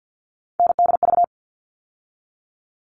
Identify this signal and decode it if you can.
Morse code, heard as a keyed tone, about 50 wpm, 700 Hz. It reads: NDV
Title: All Cut Numbers - 3-Digit 50wpm